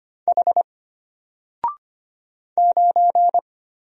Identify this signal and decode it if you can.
Morse code, heard as a keyed tone, about 25 wpm, 700 Hz. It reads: H 9
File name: Single-Letter-Number-25wpm